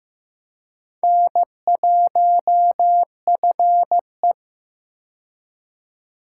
Morse code, heard as a keyed tone, about 16 wpm, 700 Hz. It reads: N1FE
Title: K1USN SST QSO Practice - 15wpm